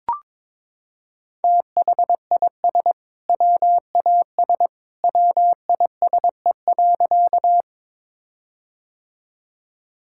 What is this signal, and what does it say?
Morse code, heard as a keyed tone, about 22 wpm, 700 Hz. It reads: THIS WAS WISE.
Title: Sentences from Top 1600 Words 22wpm